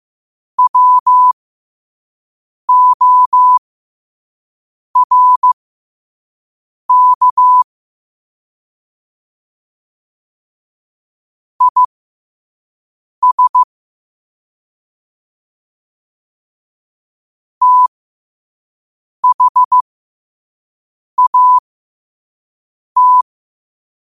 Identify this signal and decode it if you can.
Morse code, heard as a keyed tone, about 15 wpm, spaced as for 5 wpm, 1000 Hz. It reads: WORK IS THAT
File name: qotdfiles / 5wpm